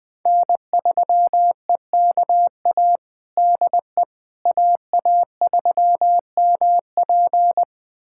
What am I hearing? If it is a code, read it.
Morse code, heard as a keyed tone, about 21 wpm, 700 Hz. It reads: N3EKA DE AA3MP
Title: USA Callsign DE Callsign - 5 Characters - 20wpm